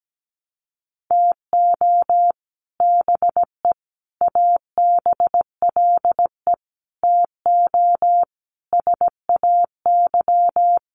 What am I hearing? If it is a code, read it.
Morse code, heard as a keyed tone, about 17 wpm, 700 Hz. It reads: TO BE ABLE TO SAY
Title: Sets of 5 Words Encore2 17wpm